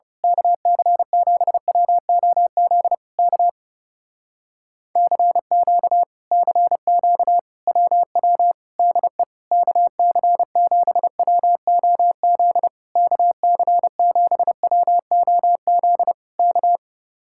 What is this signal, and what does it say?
Morse code, heard as a keyed tone, about 28 wpm, 700 Hz. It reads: KC7WOZ K CQ CQ WW DE KC7WOZ KC7WOZ K
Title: Speed-Racing Common QSO Phrases Encore - 30wpm